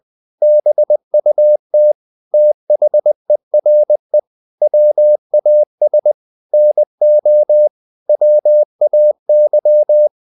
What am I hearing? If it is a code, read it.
Morse code, heard as a keyed tone, about 20 wpm, 600 Hz. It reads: BUT THERE WAS NO WAY
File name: the old man and the sea part 2 20wpm